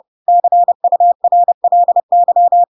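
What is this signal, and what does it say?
Morse code, heard as a keyed tone, about 30 wpm, 700 Hz. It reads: CURLY
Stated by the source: Chapter 2 of The Adventures of Pinocchio read as a book in Morse code (CW) at 30 wpm, by ik1hge.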